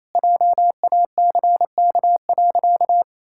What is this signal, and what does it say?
Morse code, heard as a keyed tone, about 28 wpm, 700 Hz. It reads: JACK.
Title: Pangram Challenge v2 - 28wpm